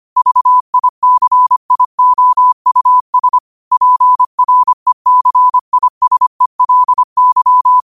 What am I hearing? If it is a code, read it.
Morse code, heard as a keyed tone, about 25 wpm, 1000 Hz. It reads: VICIOUS PRECISELY
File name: qotdfiles / 25wpm